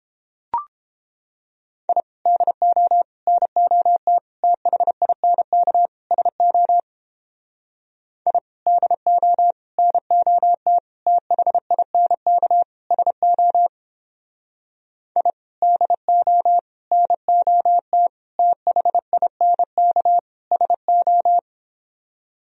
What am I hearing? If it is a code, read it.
Morse code, heard as a keyed tone, about 29 wpm, 700 Hz. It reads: I DO NOT THINK SO I DO NOT THINK SO I DO NOT THINK SO